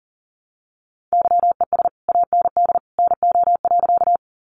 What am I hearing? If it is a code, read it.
Morse code, heard as a keyed tone, about 40 wpm, 700 Hz. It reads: YES AND NO.
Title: Sentences from Top 500 Words 40wpm